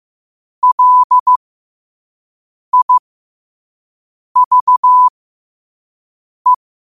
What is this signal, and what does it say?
Morse code, heard as a keyed tone, about 15 wpm, spaced as for 5 wpm, 1000 Hz. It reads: LIVE
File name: qotdfiles / 5wpm